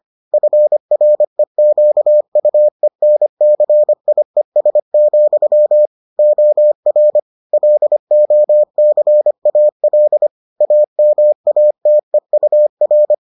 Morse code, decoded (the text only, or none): FREQUENCIES, OR LOCAL AMATEUR